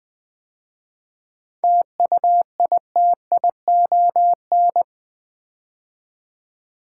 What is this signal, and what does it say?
Morse code, heard as a keyed tone, about 20 wpm, 700 Hz. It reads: TUITION
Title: 7 Letter Words - First 3 Letters Given - 20wpm